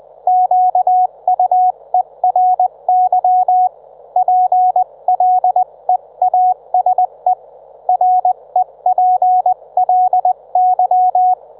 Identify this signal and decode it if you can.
Morse code, heard as a keyed tone, about 20 wpm, 730 Hz, with radio noise. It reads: QUERY PLEASE REPLY
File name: Trivia